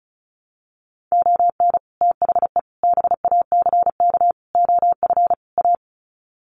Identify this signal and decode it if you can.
Morse code, heard as a keyed tone, about 34 wpm, 700 Hz. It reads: ON THE BACK OF A
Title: Sets of 5 Words Encore2 35wpm